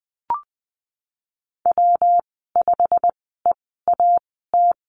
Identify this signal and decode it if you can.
Morse code, heard as a keyed tone, about 20 wpm, spaced as for 15 wpm, 700 Hz. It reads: W5EAT
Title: 179 Lesson 29g - Slash - Call Signs 20wpm